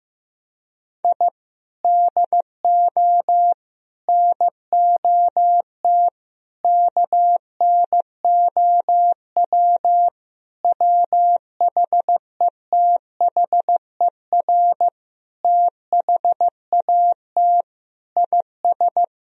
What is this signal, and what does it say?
Morse code, heard as a keyed tone, about 16 wpm, 700 Hz. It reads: I DO NOT KNOW WHETHER THAT IS
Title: Sentences from Top 1000 Words - Review Part 3 - 15wpm